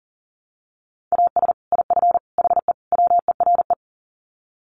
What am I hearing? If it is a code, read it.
Morse code, heard as a keyed tone, about 40 wpm, 700 Hz. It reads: AS IF HE WERE
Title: Sets of 4 Words 40wpm